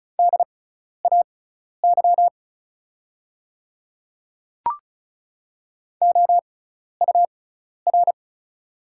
Morse code, heard as a keyed tone, about 34 wpm, spaced as for 12 wpm, 700 Hz. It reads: DAY OUR
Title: 6x Character Spacing - Top 200 Words - 35wpm